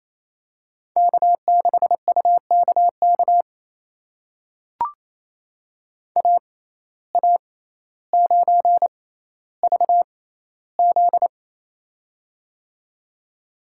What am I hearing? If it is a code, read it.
Morse code, heard as a keyed tone, about 28 wpm, spaced as for 14 wpm, 700 Hz. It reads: K6UKK AA9VZ
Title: USA Call Signs - 5 Characters - 6x Character Spacing - 28wpm